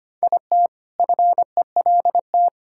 Morse code, heard as a keyed tone, about 25 wpm, 700 Hz. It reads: IT FELT
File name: practice / 25wpm / All-Age Appropriate Jokes - 25wpm